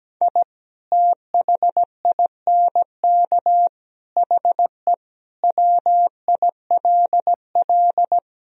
Morse code, heard as a keyed tone, about 17 wpm, 700 Hz. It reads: I THINK HE WILL
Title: Sentences from Top 1000 Words - Review Part 3 - 17wpm